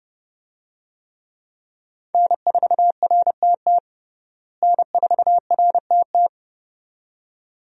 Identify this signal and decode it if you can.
Morse code, heard as a keyed tone, about 30 wpm, 700 Hz. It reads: N4RTT N4RTT
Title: USA Call Signs - 5 Characters - Repeated 2x - 30wpm